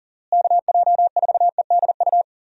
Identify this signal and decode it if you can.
Morse code, heard as a keyed tone, about 40 wpm, 700 Hz. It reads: KJ4EDU
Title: USA Call Signs - 6 Characters - Repeated 3x - 40wpm